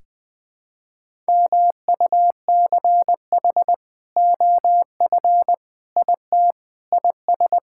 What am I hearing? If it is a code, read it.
Morse code, heard as a keyed tone, about 21 wpm, 700 Hz. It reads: MUCH OF IT IS
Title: Sentences with 2-Word Phrases - 20wpm